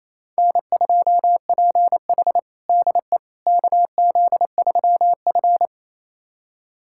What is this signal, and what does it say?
Morse code, heard as a keyed tone, about 28 wpm, 700 Hz. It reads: N2PH DE KZ3F
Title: USA Callsign DE Callsign - 4 Characters - 28wpm